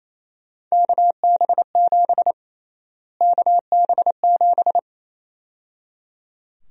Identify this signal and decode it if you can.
Morse code, heard as a keyed tone, about 28 wpm, 700 Hz. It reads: KB7 KB7
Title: USA Call Sign Prefixes - Repeated 2x - 28wpm